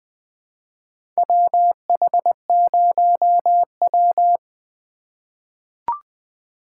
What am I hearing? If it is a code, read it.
Morse code, heard as a keyed tone, about 20 wpm, 700 Hz. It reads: WH0W
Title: USA Call Signs - 4 Characters - 2x Character Spacing - 20wpm